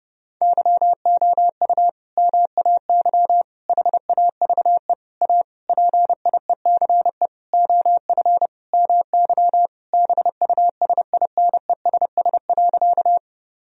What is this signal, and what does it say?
Morse code, heard as a keyed tone, about 30 wpm, 700 Hz. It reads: YOU MAY HAVE A PIECE OF MY BUSINESS.
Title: Sentences from Top 1000 Words - Review Part 2 - 30wpm